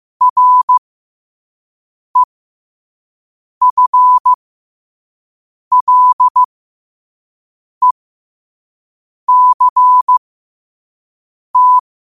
Morse code, heard as a keyed tone, about 15 wpm, spaced as for 5 wpm, 1000 Hz. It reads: REFLECT
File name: qotdfiles / 5wpm